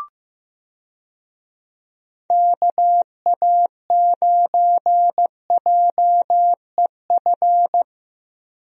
Morse code, heard as a keyed tone, about 15 wpm, 700 Hz. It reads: KA9JEF
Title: USA Call Signs - 6 Characters - Repeated 2x - 15wpm